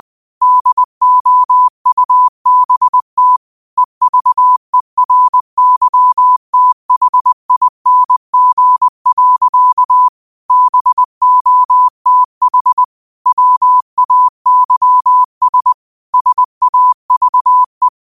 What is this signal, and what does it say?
Morse code, heard as a keyed tone, about 20 wpm, 1000 Hz. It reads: DOUBT EVERYTHING. BOTH WAYS SAVE
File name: qotdfiles / 20wpm